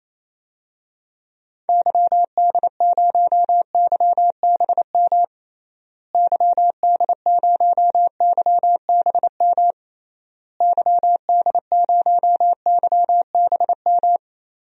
Morse code, heard as a keyed tone, about 28 wpm, 700 Hz. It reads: YD0YBM YD0YBM YD0YBM
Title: DXCC-100 Call Signs - 6 Characters - Repeated 3x - 28wpm